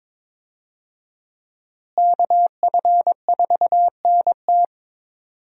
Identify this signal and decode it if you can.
Morse code, heard as a keyed tone, about 22 wpm, 700 Hz. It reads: KF4NT
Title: USA Call Signs - 5 Characters - Repeated 3x - 22wpm